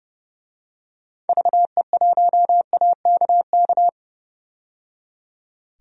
Morse code, heard as a keyed tone, about 30 wpm, 700 Hz. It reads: VE1AKK
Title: DXCC-100 Call Signs - 6 Characters - 30wpm